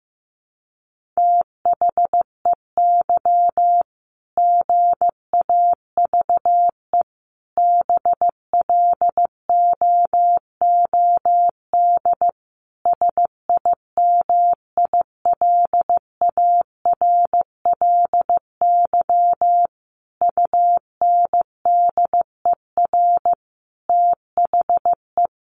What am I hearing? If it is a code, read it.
Morse code, heard as a keyed tone, about 15 wpm, 700 Hz. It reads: THEY GAVE BLOOD SIMILARLY UNDER THE
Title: Sentences from Top 1400 Words 15wpm